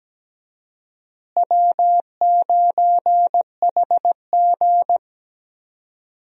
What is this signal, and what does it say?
Morse code, heard as a keyed tone, about 17 wpm, 700 Hz. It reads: W9HG
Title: USA Call Signs - 4 Characters - Repeated 3x - 17wpm